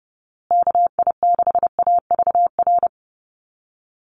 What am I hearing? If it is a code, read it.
Morse code, heard as a keyed tone, about 30 wpm, 700 Hz. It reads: KI6AVR
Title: USA Call Signs Repeated 2x 30wpm